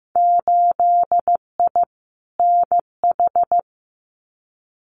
Morse code, heard as a keyed tone, about 15 wpm, 700 Hz. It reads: 8I NH
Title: Winter Field Day Exchanges Encore 15wpm